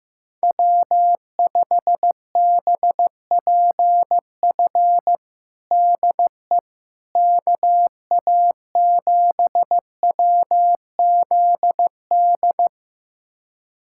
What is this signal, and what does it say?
Morse code, heard as a keyed tone, about 15 wpm, 700 Hz. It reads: W5BPF DE KA7WZD
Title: USA Callsign DE Callsign 15wpm